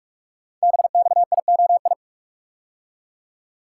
Morse code, heard as a keyed tone, about 44 wpm, 700 Hz. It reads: BXIOI
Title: Code Groups - 5 Letters - 45wpm